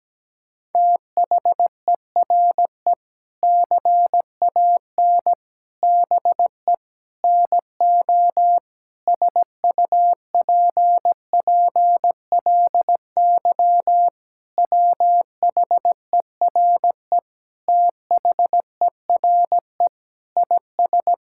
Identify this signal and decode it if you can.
Morse code, heard as a keyed tone, about 17 wpm, 700 Hz. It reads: THERE CAN BE NO SUPPLY WHERE THERE IS
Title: Sentences from Top 1000 Words - Review Part 5 - 17wpm